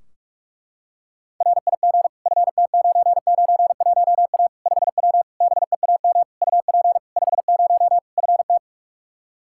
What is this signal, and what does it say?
Morse code, heard as a keyed tone, about 46 wpm, 700 Hz. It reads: RIG FT991A 5W BEAM UP 50 FT
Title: Practice QSO - Exchange 2 - Info - 45wpm